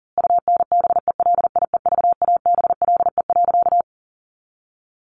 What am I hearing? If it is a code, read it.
Morse code, heard as a keyed tone, about 40 wpm, 700 Hz. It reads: UNBELIEVABLE.
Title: A Sound Of Thunder 40wpm